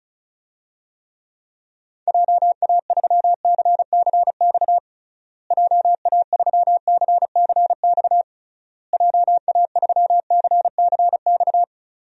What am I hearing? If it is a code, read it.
Morse code, heard as a keyed tone, about 34 wpm, 700 Hz. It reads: JA3CCX JA3CCX JA3CCX
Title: DXCC-100 Call Signs - 6 Characters - Repeated 3x - 35wpm